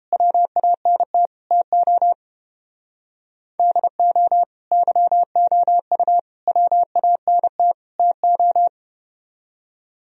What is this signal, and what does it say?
Morse code, heard as a keyed tone, about 29 wpm, 700 Hz. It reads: WANT TO DO YOU WANT TO